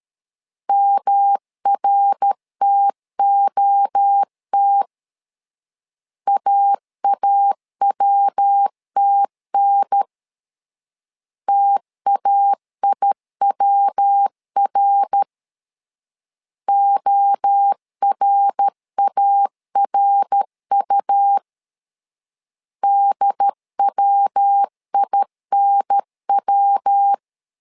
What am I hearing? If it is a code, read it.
Morse code, heard as a keyed tone, about 13 wpm, 800 Hz. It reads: MRTOT AAWTN TAIWR ORARU DWINW
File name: LEZIONI / 3 MWRO / 3 MWRO 12wpm 1730